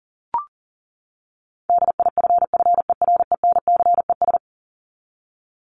Morse code, heard as a keyed tone, about 40 wpm, 700 Hz. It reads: DIFFERENCES
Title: ES:S Suffix - Top 100 Words with Base Words 40wpm